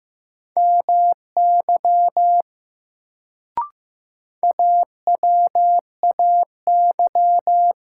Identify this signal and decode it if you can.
Morse code, heard as a keyed tone, about 15 wpm, 700 Hz. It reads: MY AWAY
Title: Sets of 2 Words - New Word Drill - 15wpm